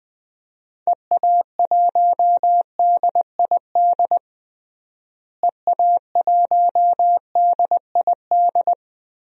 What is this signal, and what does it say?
Morse code, heard as a keyed tone, about 20 wpm, 700 Hz. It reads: EA1DID EA1DID
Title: DXCC-100 Call Signs - 6 Characters - Repeated 2x - 20wpm